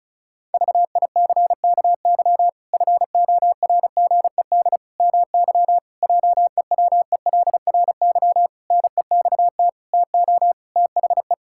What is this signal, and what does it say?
Morse code, heard as a keyed tone, about 34 wpm, 700 Hz. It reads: VICKY FORGED MY JEWELRY NEXT TO THE
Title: Pangram Challenge v2 - 35wpm